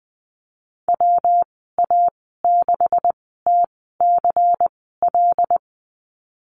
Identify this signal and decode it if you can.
Morse code, heard as a keyed tone, about 19 wpm, spaced as for 14 wpm, 700 Hz. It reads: WA6TCL
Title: 179 Lesson 29g - Slash - Call Signs 20wpm